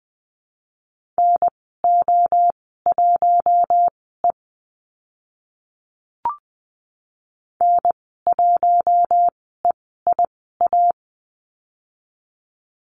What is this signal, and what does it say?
Morse code, heard as a keyed tone, about 20 wpm, spaced as for 14 wpm, 700 Hz. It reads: NO1E N1EIA
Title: 042 Lesson 8g - Number 1 - Call Signs 20wpm